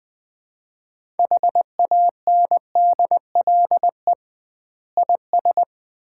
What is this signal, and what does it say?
Morse code, heard as a keyed tone, about 20 wpm, 700 Hz. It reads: HANDLE IS
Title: Common QSO Phrases - 2x Word-Spacing - 20wpm